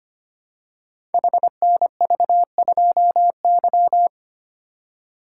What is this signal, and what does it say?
Morse code, heard as a keyed tone, about 25 wpm, 700 Hz. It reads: HNV2Y
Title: Code Groups - 5 Letter:Numbers - 25wpm